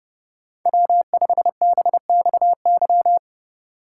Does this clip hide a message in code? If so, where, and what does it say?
Yes: Morse code from 0.7 to 3.2 s, heard as a keyed tone, about 31 wpm, 700 Hz. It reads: W5BXY